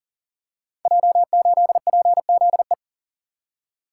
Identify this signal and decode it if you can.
Morse code, heard as a keyed tone, about 41 wpm, 700 Hz. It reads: J8PZE
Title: Code Groups - 5 Letter:Numbers - 40wpm